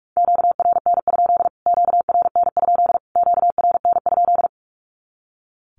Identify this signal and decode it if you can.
Morse code, heard as a keyed tone, about 44 wpm, 700 Hz. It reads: QRN? QRN? QRN?
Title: Q-Codes Repeated 3x 45wpm